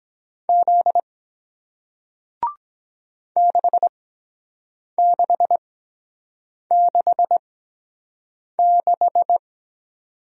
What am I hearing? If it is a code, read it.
Morse code, heard as a keyed tone, about 23 wpm, 700 Hz. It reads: Z 6 6 6 6